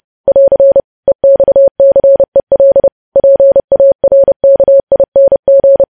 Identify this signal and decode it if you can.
Morse code, heard as a keyed tone, about 30 wpm, 550 Hz. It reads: <AR> EXCEL PARKING
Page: news headlines in morse code at 30 wpm